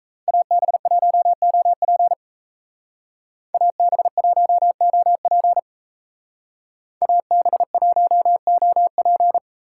Morse code, heard as a keyed tone, about 33 wpm, 700 Hz. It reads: AB1OP AB1OP AB1OP
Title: Speed Racing - USA 2x2 Call Signs 28wpm